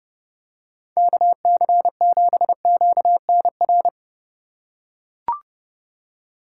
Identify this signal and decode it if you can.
Morse code, heard as a keyed tone, about 30 wpm, 700 Hz. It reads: KC7QNR
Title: USA Call Signs - 6 Characters - 3x Character Spacing - 30wpm